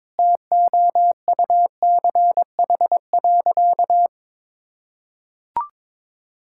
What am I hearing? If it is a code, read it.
Morse code, heard as a keyed tone, about 22 wpm, 700 Hz. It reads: TOUCH.
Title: Sentences from Top 1600 Words 22wpm